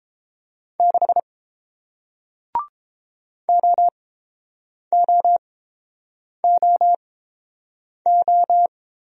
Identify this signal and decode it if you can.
Morse code, heard as a keyed tone, about 30 wpm, spaced as for 7 wpm, 700 Hz. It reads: 6 OOOO